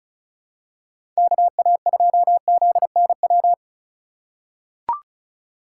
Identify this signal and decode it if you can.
Morse code, heard as a keyed tone, about 36 wpm, 700 Hz. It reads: KA2ZNW
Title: USA Call Signs - 6 Characters - 3x Character Spacing - 35wpm